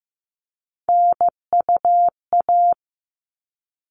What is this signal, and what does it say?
Morse code, heard as a keyed tone, about 15 wpm, 700 Hz. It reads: NUA
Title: All Cut Numbers - 3-Digit 15wpm